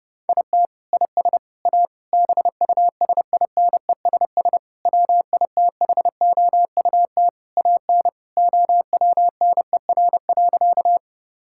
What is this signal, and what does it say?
Morse code, heard as a keyed tone, about 30 wpm, 700 Hz. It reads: IT IS A BUSINESS WITHOUT AN OWNER.
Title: Sentences from Top 1600 Words 30wpm